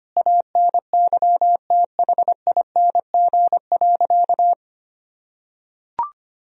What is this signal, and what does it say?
Morse code, heard as a keyed tone, about 25 wpm, 700 Hz. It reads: ANYTHING.
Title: Sentences from Top 1000 Words - Review Part 3 - 25wpm